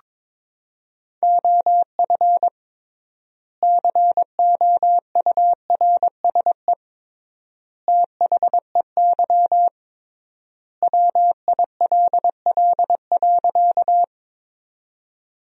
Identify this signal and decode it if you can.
Morse code, heard as a keyed tone, about 22 wpm, 700 Hz. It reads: OF COURSE THEY WILL.
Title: 3x Word-Spacing - Sentences from Top 500 Words - 22wpm